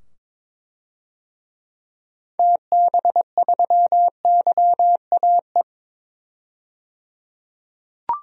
Morse code, heard as a keyed tone, about 21 wpm, 700 Hz. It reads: TB3YAE
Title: DXCC-100 Call Signs - 6 Characters - 22wpm